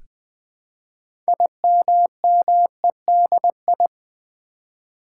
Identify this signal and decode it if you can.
Morse code, heard as a keyed tone, about 20 wpm, 700 Hz. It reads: IMMEDI
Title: Build A Word - 3 Letters At A Time - 9-Letter Words - 20wpm